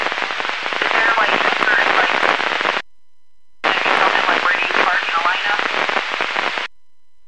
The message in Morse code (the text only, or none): none